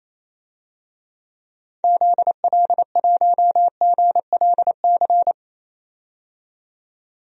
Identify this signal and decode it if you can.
Morse code, heard as a keyed tone, about 28 wpm, 700 Hz. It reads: ZL1GLC
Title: DXCC-100 Call Signs - 6 Characters - 28wpm